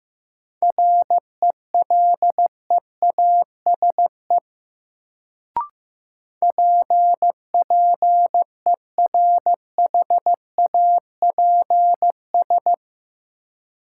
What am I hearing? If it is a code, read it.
Morse code, heard as a keyed tone, about 15 wpm, 700 Hz. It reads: RELEASE PPERHAPS
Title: Sentences from Top 1100 Words - New Word Drill - 15wpm